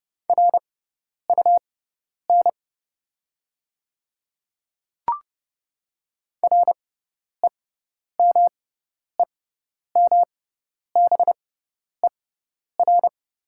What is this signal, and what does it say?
Morse code, heard as a keyed tone, about 30 wpm, spaced as for 10 wpm, 700 Hz. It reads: RUN REMEMBER
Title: 6x Character Spacing - Top 500 Words - 30wpm